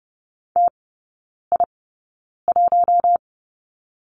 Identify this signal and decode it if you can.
Morse code, heard as a keyed tone, about 30 wpm, spaced as for 9 wpm, 700 Hz. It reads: TI1
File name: Lesson 17e - Letter u - Three Characters 30wpm